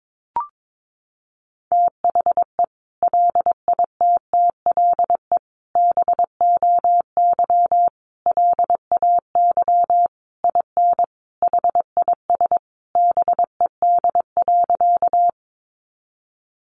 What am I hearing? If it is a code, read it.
Morse code, heard as a keyed tone, about 22 wpm, 700 Hz. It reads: THE LITTLE BOY LAY IN HIS BED.
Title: Sentences from Top 500 Words 22wpm